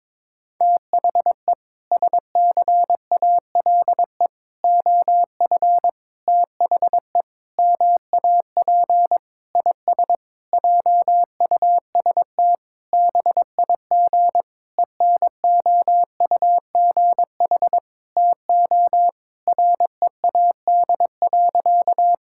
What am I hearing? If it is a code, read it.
Morse code, heard as a keyed tone, about 22 wpm, 700 Hz. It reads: THE SCALE OF THE MAP IS JUST BIG ENOUGH TO READ.
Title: Sentences from Top 1000 Words - Review Part 5 - 22wpm